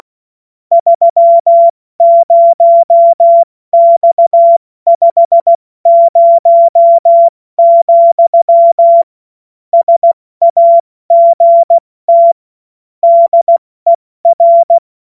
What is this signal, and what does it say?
Morse code, heard as a keyed tone, about 16 wpm, 680 Hz. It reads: 30X50, SAGT DER
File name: Morsekurs / Witze / Witz 20. Januar 2026 70 F80